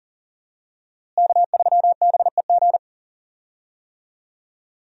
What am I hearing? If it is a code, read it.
Morse code, heard as a keyed tone, about 38 wpm, 700 Hz. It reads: K3BEG
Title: USA Call Signs - 5 Characters - Repeated 2x - 40wpm